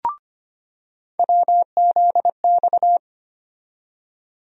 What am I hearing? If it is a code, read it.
Morse code, heard as a keyed tone, about 25 wpm, 700 Hz. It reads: WZX
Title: USA Call Sign Postfixes - 3 Letters 25wpm